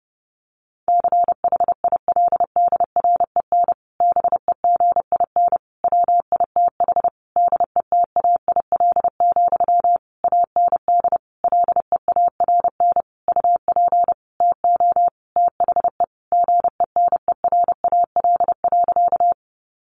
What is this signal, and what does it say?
Morse code, heard as a keyed tone, about 30 wpm, 700 Hz. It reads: CHILDREN BEGIN WITH DETAIL, AND LEARN UP TO THE GENERAL.
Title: Sentences from Top 1500 Words 30wpm